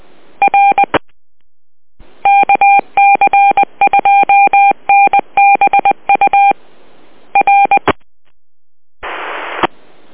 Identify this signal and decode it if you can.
Morse code, heard as a keyed tone, about 20 wpm, 800 Hz. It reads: R KC2NBU R